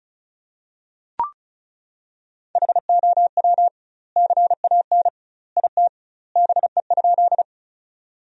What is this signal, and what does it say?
Morse code, heard as a keyed tone, about 34 wpm, 700 Hz. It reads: HOW CAN IT BE?
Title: 2x Word-Spacing - Sentences from Top 100 Words - 35wpm